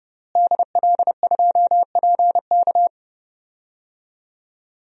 Morse code, heard as a keyed tone, about 32 wpm, 700 Hz. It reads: DL2PK
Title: DXCC-100 Call Signs - 5 Characters - Repeated 3x - 30wpm